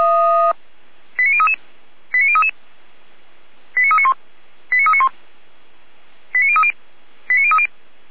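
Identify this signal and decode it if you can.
Five-tone selective call (ZVEI), heard as quick runs of five tones, tones of about 80 ms. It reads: 89033 89033 89391 89391 89030 89030
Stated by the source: Received on an SDR.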